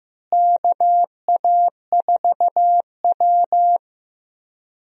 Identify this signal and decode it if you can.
Morse code, heard as a keyed tone, about 15 wpm, 700 Hz. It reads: KA4W
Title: USA Call Signs - 4 Characters - Repeated 3x - 15wpm